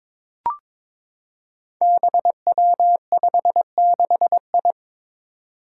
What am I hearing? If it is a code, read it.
Morse code, heard as a keyed tone, about 22 wpm, 700 Hz. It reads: BW56I
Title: Five Challenging Letter-Number Combinations 22wpm